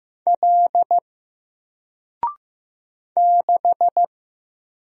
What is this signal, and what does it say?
Morse code, heard as a keyed tone, about 15 wpm, 700 Hz. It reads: L 6
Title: Single Character (Letter:Number:Punctuation:Prosign) 15wpm